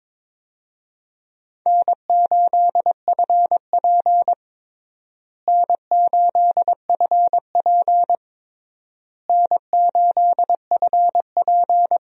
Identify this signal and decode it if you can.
Morse code, heard as a keyed tone, about 22 wpm, 700 Hz. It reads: N8FP N8FP N8FP
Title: USA Call Signs - 4 Characters - Repeated 3x - 22wpm